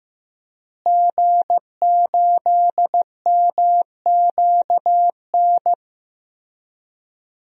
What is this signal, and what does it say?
Morse code, heard as a keyed tone, about 15 wpm, 700 Hz. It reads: G8MQN
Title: DXCC-100 Call Signs - 5 Characters - Repeated 3x - 15wpm